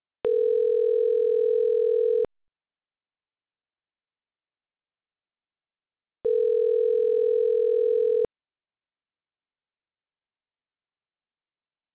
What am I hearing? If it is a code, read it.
Ringback tone (North America), 440 + 480 Hz.